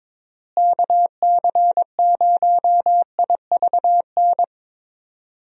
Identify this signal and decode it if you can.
Morse code, heard as a keyed tone, about 21 wpm, 700 Hz. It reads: KC0IVN
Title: USA Call Signs - 6 Characters - Repeated 2x - 22wpm